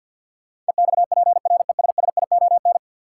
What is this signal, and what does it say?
Morse code, heard as a keyed tone, about 51 wpm, 700 Hz. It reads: EXPRESSION
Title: Sentences from Top 1200 Words - New Word Drill Alt - 50wpm